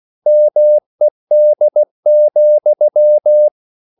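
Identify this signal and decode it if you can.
Morse code, heard as a keyed tone, about 16 wpm, 600 Hz. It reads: MED,
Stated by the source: Slow (QRS) Morse code at 16 wpm, from CW med Gnister.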